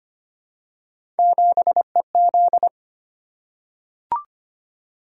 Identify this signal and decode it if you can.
Morse code, heard as a keyed tone, about 25 wpm, 700 Hz. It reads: 7EZ
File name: Three Letter-Number Combinations - 25wpm